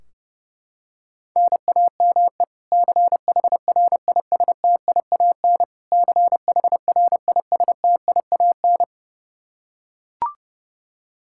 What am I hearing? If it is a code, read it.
Morse code, heard as a keyed tone, about 32 wpm, 700 Hz. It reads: NAME CHRISTIAN CHRISTIAN
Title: Practice QSO - Exchange 1 - Beginning - 30wpm